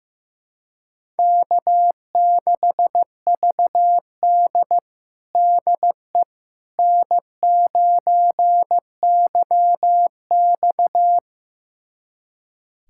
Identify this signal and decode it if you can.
Morse code, heard as a keyed tone, about 15 wpm, 700 Hz. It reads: K6VD DE N9YX
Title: USA Callsign DE Callsign - 4 Characters - 15wpm